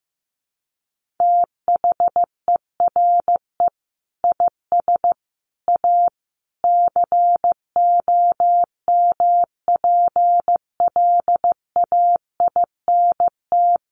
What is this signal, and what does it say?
Morse code, heard as a keyed tone, about 15 wpm, 700 Hz. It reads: THERE IS A COMPLAINT
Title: Sentences from Top 1400 Words 15wpm